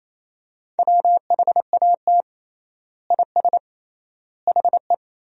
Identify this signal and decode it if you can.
Morse code, heard as a keyed tone, about 28 wpm, 700 Hz. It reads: WHAT IS HE
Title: 3x Word-Spacing - Sentences from Top 100 Words - 28wpm